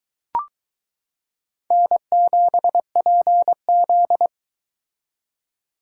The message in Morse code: N7PZ

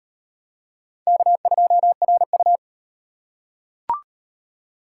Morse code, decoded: K2RU

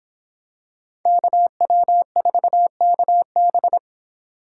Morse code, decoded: KW4KB

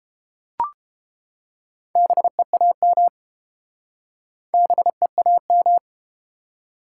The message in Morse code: BEAM BEAM

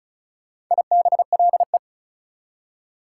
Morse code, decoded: IBLE

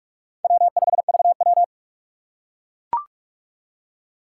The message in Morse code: W5VW

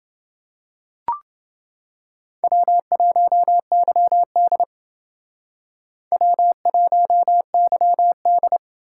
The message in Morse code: W1YD W1YD

